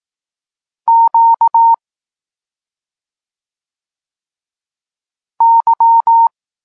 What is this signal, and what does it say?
Morse code, heard as a keyed tone, about 18 wpm, 920 Hz. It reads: Q Y
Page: Tekst 32 – Doskonalenie odbioru poznanych liter